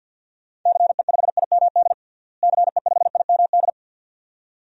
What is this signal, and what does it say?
Morse code, heard as a keyed tone, about 52 wpm, 700 Hz. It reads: CE5IMD CE5IMD
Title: DXCC-100 Call Signs - 6 Characters - Repeated 2x - 50wpm